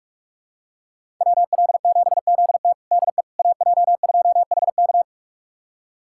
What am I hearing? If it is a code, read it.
Morse code, heard as a keyed tone, about 41 wpm, 700 Hz. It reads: WL7ZT DE AJ2HK
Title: USA Callsign DE Callsign - 5 Characters - 45wpm